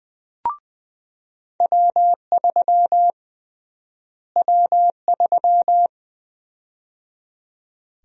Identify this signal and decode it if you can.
Morse code, heard as a keyed tone, about 20 wpm, 700 Hz. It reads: W3 W3
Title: USA Call Sign Prefixes - Repeated 2x - 20wpm